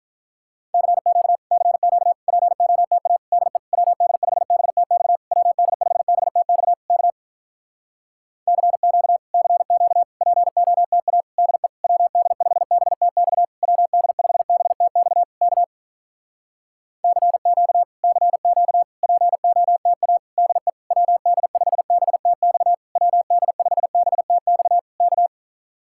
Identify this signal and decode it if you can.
Morse code, heard as a keyed tone, about 47 wpm, 700 Hz. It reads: CQ CQ POTA DE WD5BTX WD5BTX K CQ CQ POTA DE WD5BTX WD5BTX K CQ CQ POTA DE WD5BTX WD5BTX K